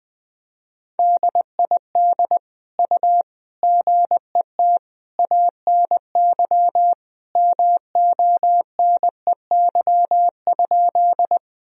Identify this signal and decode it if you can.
Morse code, heard as a keyed tone, about 20 wpm, 700 Hz. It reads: DID U GET ANY MONEY?
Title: Sentences from Top 400 Words and Abbreviations 20wpm